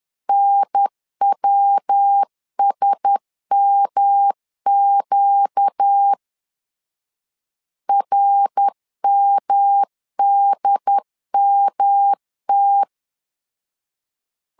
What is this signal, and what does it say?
Morse code, heard as a keyed tone, about 10 wpm, 800 Hz. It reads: NWSMQ RMDMT
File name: LEZIONI / 4 Q1S 10wpm 2133